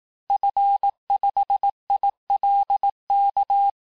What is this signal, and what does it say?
Morse code, heard as a keyed tone, about 18 wpm, 800 Hz. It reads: F5ILK
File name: cours-du-lundi files / cours-1852-FIN-du-QSO-vitesse-17MPM